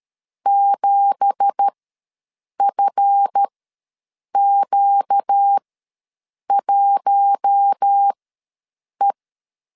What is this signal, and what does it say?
Morse code, heard as a keyed tone, about 13 wpm, spaced as for 7 wpm, 800 Hz. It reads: 7FQ1E
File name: LEZIONI / 9 2867 ws 12wpm 3431